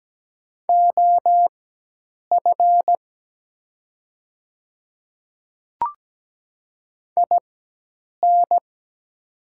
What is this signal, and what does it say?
Morse code, heard as a keyed tone, about 17 wpm, spaced as for 8 wpm, 700 Hz. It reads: OF IN
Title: 4x Character Spacing - Top 100 Words - 17wpm